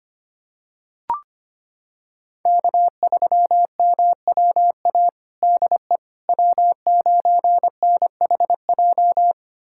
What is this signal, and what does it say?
Morse code, heard as a keyed tone, about 25 wpm, 700 Hz. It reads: K3MWA DE W9NHJ
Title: USA Callsign DE Callsign - 5 Characters - 25wpm